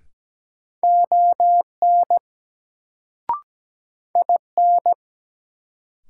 Morse code, heard as a keyed tone, about 17 wpm, 700 Hz. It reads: ON IN
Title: Sets of 2 Words - New Word Drill - 17wpm